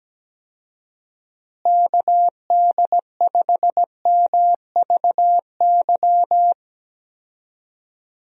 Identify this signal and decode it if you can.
Morse code, heard as a keyed tone, about 17 wpm, 700 Hz. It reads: KD5MVY
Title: USA Call Signs - 6 Characters - Repeated 2x - 17wpm